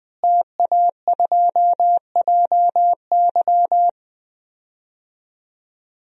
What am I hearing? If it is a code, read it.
Morse code, heard as a keyed tone, about 20 wpm, 700 Hz. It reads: TA2JY
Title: DXCC-100 Call Signs - 5 Characters - Repeated 2x - 20wpm